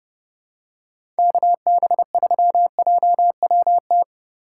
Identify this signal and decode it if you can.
Morse code, heard as a keyed tone, about 30 wpm, 700 Hz. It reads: KB3JWT
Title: USA Call Signs - 3x Character Spacing - 30wpm